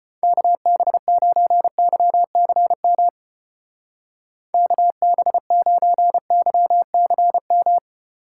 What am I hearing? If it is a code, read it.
Morse code, heard as a keyed tone, about 32 wpm, 700 Hz. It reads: KB9YCM KB9YCM